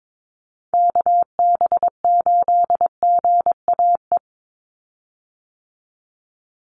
Morse code, heard as a keyed tone, about 22 wpm, 700 Hz. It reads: KB8GAE
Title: Call Signs - 22wpm